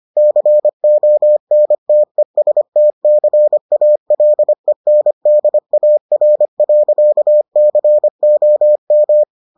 Morse code, heard as a keyed tone, about 25 wpm, 600 Hz. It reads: CONTESTCALENDAR.COM